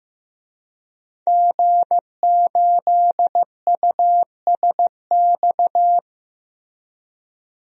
Morse code, heard as a keyed tone, about 15 wpm, 700 Hz. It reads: G8USX
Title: DXCC-100 Call Signs - 5 Characters - Repeated 3x - 15wpm